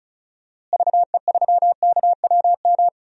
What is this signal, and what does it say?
Morse code, heard as a keyed tone, about 34 wpm, 700 Hz. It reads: VE3KWM
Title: DXCC-100 Call Signs - 6 Characters - Repeated 3x - 35wpm